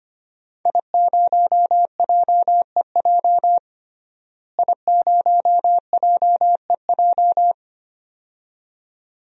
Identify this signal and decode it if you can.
Morse code, heard as a keyed tone, about 25 wpm, 700 Hz. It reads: I0JEJ I0JEJ
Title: DXCC-100 Call Signs - 5 Characters - Repeated 2x - 25wpm